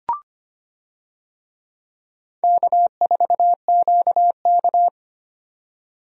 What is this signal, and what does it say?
Morse code, heard as a keyed tone, about 25 wpm, 700 Hz. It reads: K4QK
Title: USA Call Signs - 4 Characters - 25wpm